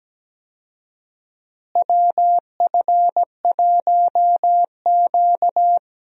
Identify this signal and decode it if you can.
Morse code, heard as a keyed tone, about 17 wpm, 700 Hz. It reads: WF1Q
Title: USA Call Signs - 4 Characters - Repeated 3x - 17wpm